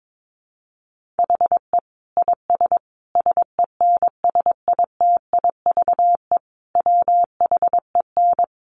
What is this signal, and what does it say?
Morse code, heard as a keyed tone, about 22 wpm, 700 Hz. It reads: HE IS SENSITIVE WHEN
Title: Sentences from Top 1400 Words 22wpm